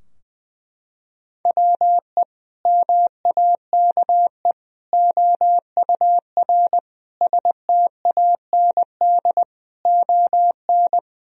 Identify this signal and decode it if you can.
Morse code, heard as a keyed tone, about 21 wpm, 700 Hz. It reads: WE MAKE OUR STAND ON
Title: Sentences from Top 1000 Words - Review Part 2 - 20wpm